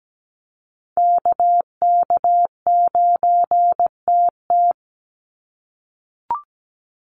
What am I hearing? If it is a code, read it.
Morse code, heard as a keyed tone, about 17 wpm, 700 Hz. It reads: KK9TT
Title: USA Call Signs 17wpm